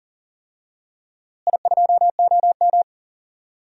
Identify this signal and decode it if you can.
Morse code, heard as a keyed tone, about 40 wpm, 700 Hz. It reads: I2OM
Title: DXCC-100 Call Signs - 4 Characters - Repeated 3x - 40wpm